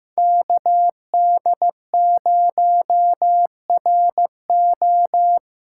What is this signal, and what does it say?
Morse code, heard as a keyed tone, about 15 wpm, 700 Hz. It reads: KD0RO
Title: USA Call Signs - 5 Characters - 2x Character Spacing - 15wpm